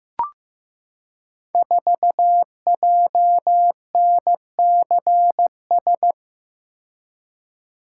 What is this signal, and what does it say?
Morse code, heard as a keyed tone, about 15 wpm, 700 Hz. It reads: 4JNCS
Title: Code Groups - 5 Letter:Numbers - 15wpm